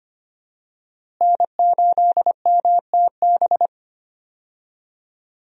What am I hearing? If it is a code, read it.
Morse code, heard as a keyed tone, about 25 wpm, 700 Hz. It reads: N8MTB
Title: USA Call Signs - 5 Characters - Repeated 3x - 25wpm